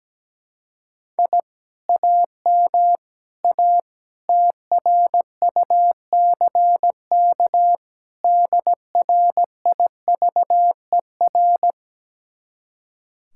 Morse code, heard as a keyed tone, about 17 wpm, 700 Hz. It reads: I AM A TRUCK DRIVER
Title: Speed-Racing Common QSO Phrases Encore - 17wpm